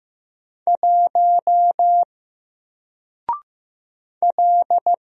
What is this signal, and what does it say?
Morse code, heard as a keyed tone, about 15 wpm, 700 Hz. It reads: 1 L
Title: Single Letter-Number Alt 15wpm